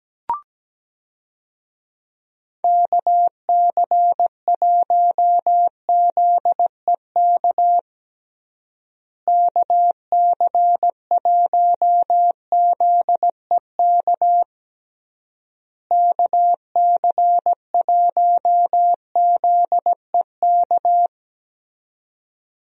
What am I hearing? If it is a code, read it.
Morse code, heard as a keyed tone, about 17 wpm, 700 Hz. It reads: KC1ZEK KC1ZEK KC1ZEK
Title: USA Call Signs - 6 Characters - Repeated 3x - 17wpm